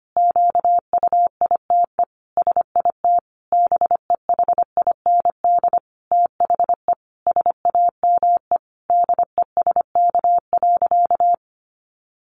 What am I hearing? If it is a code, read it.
Morse code, heard as a keyed tone, about 25 wpm, 700 Hz. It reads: QUITE SIT BEHIND THE SAME DESK.
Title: A Sound Of Thunder 25wpm